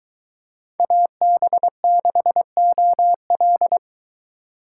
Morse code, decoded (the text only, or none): AB6OL